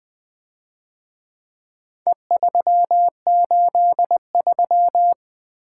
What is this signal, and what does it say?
Morse code, heard as a keyed tone, about 24 wpm, 700 Hz. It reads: E383
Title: Speed-Racing - Four Letter-Number Combinations - 20wpm